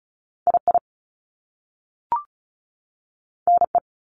Morse code, heard as a keyed tone, about 36 wpm, 700 Hz. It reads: I I N E
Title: Top Two Letter Combinations in Words 35wpm